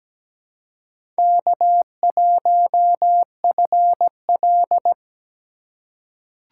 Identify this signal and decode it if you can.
Morse code, heard as a keyed tone, about 17 wpm, 700 Hz. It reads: K1FL
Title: USA Call Signs - 3x Character Spacing - 17wpm